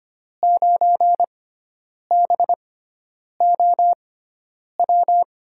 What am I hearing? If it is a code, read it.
Morse code, heard as a keyed tone, about 25 wpm, spaced as for 9 wpm, 700 Hz. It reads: 9BOW
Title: USA Call Signs - 6 Characters - 6x Character Spacing - 25wpm